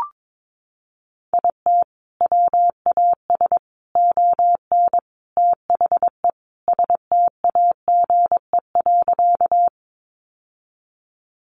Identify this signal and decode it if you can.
Morse code, heard as a keyed tone, about 22 wpm, 700 Hz. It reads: IT WAS ON THE STAGE.
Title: Sentences from Top 1200 Words 22wpm